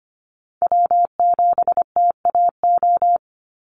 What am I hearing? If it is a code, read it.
Morse code, heard as a keyed tone, about 24 wpm, 700 Hz. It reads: W7TAO
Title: NA SOTA Chaser Callsigns 25wpm